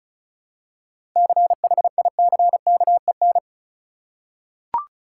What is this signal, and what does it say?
Morse code, heard as a keyed tone, about 34 wpm, 700 Hz. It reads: CHICKEN
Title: Seven Letter Words - 35wpm